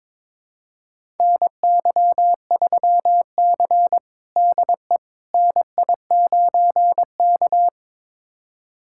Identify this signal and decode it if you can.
Morse code, heard as a keyed tone, about 22 wpm, 700 Hz. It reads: NY3C DE NI9K
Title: USA Callsign DE Callsign - 4 Characters - 22wpm